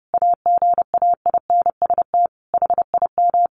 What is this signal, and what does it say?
Morse code, heard as a keyed tone, about 28 wpm, 700 Hz. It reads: AGAINST HIM
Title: Sentences from Top 1400 Words 30wpm